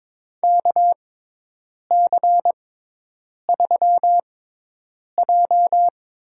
Morse code, heard as a keyed tone, about 22 wpm, spaced as for 8 wpm, 700 Hz. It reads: KC3J
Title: USA Call Signs - 6 Characters - 6x Character Spacing - 22wpm